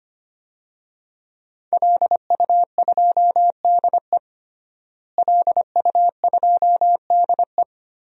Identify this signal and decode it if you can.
Morse code, heard as a keyed tone, about 25 wpm, 700 Hz. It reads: LU2DE LU2DE
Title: DXCC-100 Call Signs - 5 Characters - Repeated 3x - 25wpm